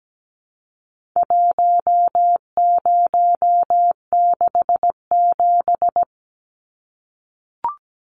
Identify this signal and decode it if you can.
Morse code, heard as a keyed tone, about 17 wpm, 700 Hz. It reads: 1067
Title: Numbers - 4-Digit 17wpm